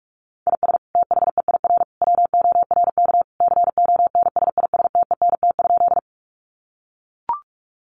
Morse code, heard as a keyed tone, about 44 wpm, 700 Hz. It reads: IS THEIR WORK CONSISTENT?
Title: Sentences from Top 1500 Words 45wpm